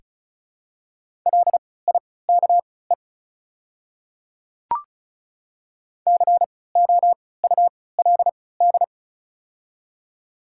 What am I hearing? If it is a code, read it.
Morse code, heard as a keyed tone, about 34 wpm, spaced as for 20 wpm, 700 Hz. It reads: LIKE COULD
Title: 3x Character Spacing - Top 200 Words - 35wpm